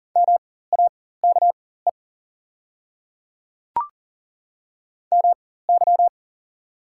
Morse code, heard as a keyed tone, about 40 wpm, spaced as for 19 wpm, 690 Hz. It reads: MAKE MY
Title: 4x Character Spacing - Top 200 Words - 40wpm